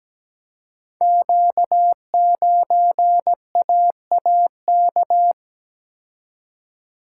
Morse code, heard as a keyed tone, about 17 wpm, 690 Hz. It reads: Q9AAK
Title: Code Groups - 5 Letter:Numbers - 17wpm